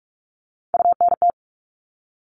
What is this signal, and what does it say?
Morse code, heard as a keyed tone, about 44 wpm, 700 Hz. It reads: UNT
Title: All Cut Numbers - 3-Digit 45wpm